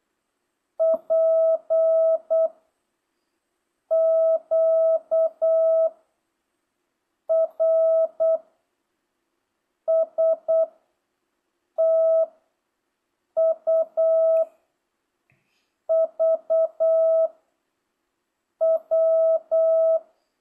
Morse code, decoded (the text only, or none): P Q R S T U V W